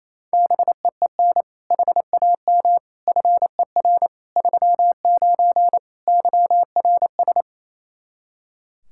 Morse code, heard as a keyed tone, about 28 wpm, 700 Hz. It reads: BEEN HAM FER 39 YRS
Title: Practice QSO - Exchange 3 - More Info - 28wpm